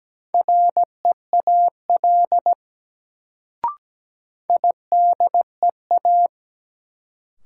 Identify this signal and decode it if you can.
Morse code, heard as a keyed tone, about 17 wpm, 700 Hz. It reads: REAL IDEA
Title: Sentences from Top 300 Words - New Word Drill - 17wpm